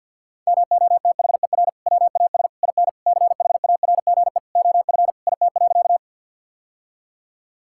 Morse code, heard as a keyed tone, about 50 wpm, 700 Hz. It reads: MOTHER WAS IN CHARGE OF IT.
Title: Sentences from Top 1000 Words - Review Part 6 - 50wpm